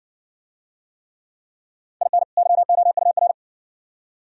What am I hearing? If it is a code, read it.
Morse code, heard as a keyed tone, about 55 wpm, 700 Hz. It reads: IN YOUR